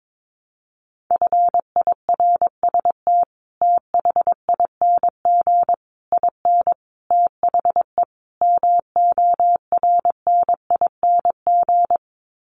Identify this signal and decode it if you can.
Morse code, heard as a keyed tone, about 22 wpm, 700 Hz. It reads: FIRST THING IN THE MORNING
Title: Sets of 5 Words Encore2 22wpm